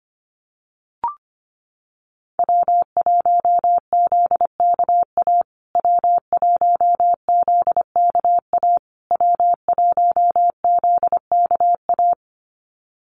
Morse code, heard as a keyed tone, about 25 wpm, 700 Hz. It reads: W1ZKA W1ZKA W1ZKA
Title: USA Call Signs Repeated 3x 25wpm